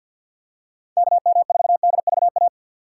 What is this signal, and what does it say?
Morse code, heard as a keyed tone, about 53 wpm, 700 Hz. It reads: KM4DVA
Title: USA Call Signs - 6 Characters - 3x Character Spacing - 50wpm